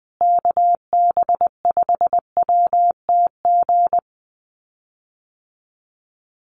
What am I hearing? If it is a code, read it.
Morse code, heard as a keyed tone, about 20 wpm, 700 Hz. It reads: KB5WTG
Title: USA Call Signs Repeated 3x 20wpm